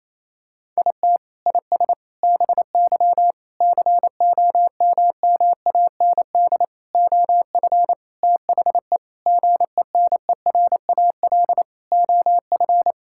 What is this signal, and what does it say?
Morse code, heard as a keyed tone, about 27 wpm, 700 Hz. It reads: IT IS BY COMMAND OF THE GENERAL OF
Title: Sentences from Top 1600 Words 28wpm